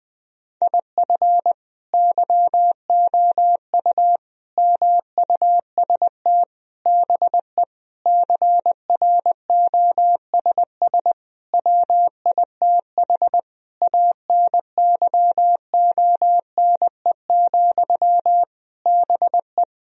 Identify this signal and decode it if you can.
Morse code, heard as a keyed tone, about 21 wpm, 700 Hz. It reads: IF YOU MUST BE CROSS WITH ANYONE, BE
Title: Sentences from Top 1000 Words - Review Part 1 - 20wpm